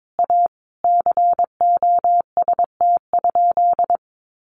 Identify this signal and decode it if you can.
Morse code, heard as a keyed tone, about 22 wpm, 700 Hz. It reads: A COST?
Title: Sentences from Top 800 Words 22wpm